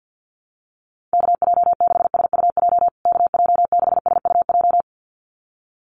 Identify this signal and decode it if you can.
Morse code, heard as a keyed tone, about 50 wpm, 700 Hz. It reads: KJ6SUJ KJ6SUJ
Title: USA Call Signs Repeated 2x 50wpm